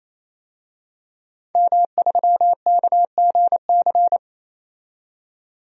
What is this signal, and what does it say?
Morse code, heard as a keyed tone, about 27 wpm, 700 Hz. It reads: M3KGC
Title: DXCC-100 Call Signs - 5 Characters - 28wpm